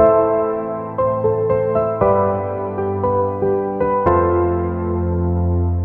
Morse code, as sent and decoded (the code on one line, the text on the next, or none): none